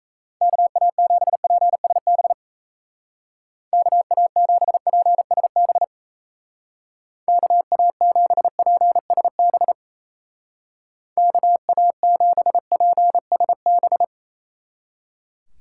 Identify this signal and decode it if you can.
Morse code, heard as a keyed tone, about 35 wpm, 700 Hz. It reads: KA7PSB KA7PSB KA7PSB KA7PSB
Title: Speed Racing - USA Call Signs 28wpm